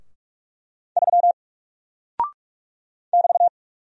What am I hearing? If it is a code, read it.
Morse code, heard as a keyed tone, about 44 wpm, 700 Hz. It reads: 3 <BT>
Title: Single Character (Letter:Number:Punctuation:Prosign) Alt 45wpm